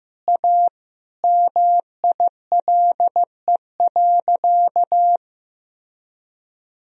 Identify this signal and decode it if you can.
Morse code, heard as a keyed tone, about 15 wpm, 700 Hz. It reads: A MILE.
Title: Sentences from Top 1000 Words - Review Part 4 - 15wpm